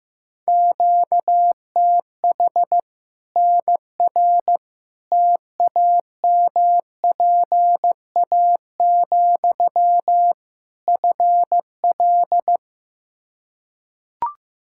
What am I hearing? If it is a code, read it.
Morse code, heard as a keyed tone, about 16 wpm, 700 Hz. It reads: QTH NR TAMPA, FL
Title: Practice QSO - Exchange 1 - Beginning - 15wpm